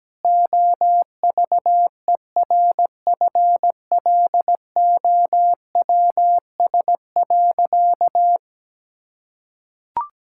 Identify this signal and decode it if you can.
Morse code, heard as a keyed tone, about 17 wpm, 700 Hz. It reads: OVERFLOWS.
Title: Pangram Challenge v2 - 17wpm